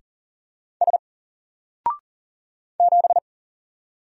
Morse code, heard as a keyed tone, about 38 wpm, 700 Hz. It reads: S 7
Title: Single Character (Letter:Number:Punctuation:Prosign) 40wpm